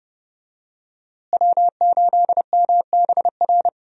Morse code, heard as a keyed tone, about 30 wpm, 700 Hz. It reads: W8MBR
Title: USA Call Signs - 5 Characters - Repeated 2x - 30wpm